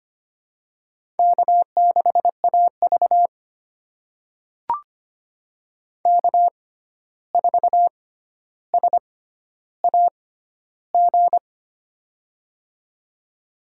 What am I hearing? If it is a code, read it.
Morse code, heard as a keyed tone, about 25 wpm, spaced as for 12 wpm, 700 Hz. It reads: K6AV K4SAG
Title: USA Call Signs - 6x Character Spacing - 25wpm